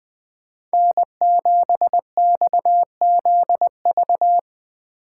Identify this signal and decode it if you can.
Morse code, heard as a keyed tone, about 20 wpm, 700 Hz. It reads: N7XZV
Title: Speed Racing - USA 1x3 Call Signs 17wpm